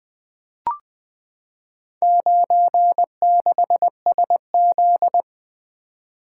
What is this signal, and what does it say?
Morse code, heard as a keyed tone, about 20 wpm, 700 Hz. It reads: 96SZ
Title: Four Letter-Number Combinations - 20wpm